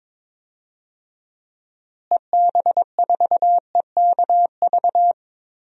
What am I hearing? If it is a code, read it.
Morse code, heard as a keyed tone, about 22 wpm, 700 Hz. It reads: EB4EKV
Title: DXCC-100 Call Signs - 6 Characters - Repeated 3x - 22wpm